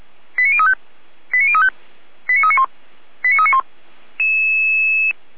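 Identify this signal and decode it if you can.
Five-tone selective call (ZVEI), heard as quick runs of five tones, tones of about 70 ms. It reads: 89035 89035 89391 89391
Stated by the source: ZVEI five-tone selective call 89509, received by SDR, from the audio.f85360.feuerwehr-orlamuende.org fire brigade server.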